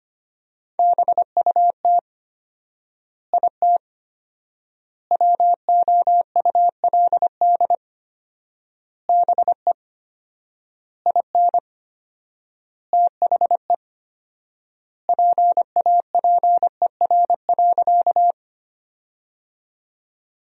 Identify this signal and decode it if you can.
Morse code, heard as a keyed tone, about 25 wpm, 700 Hz. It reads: BUT IT WOULD BE IN THE PAPER.
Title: 4x Word-Spacing - Sentences from Top 300 Words - 25wpm